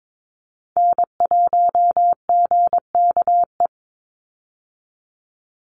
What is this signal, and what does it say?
Morse code, heard as a keyed tone, about 22 wpm, 700 Hz. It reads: N1GKE
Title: Call Signs Repeated 3x - 22wpm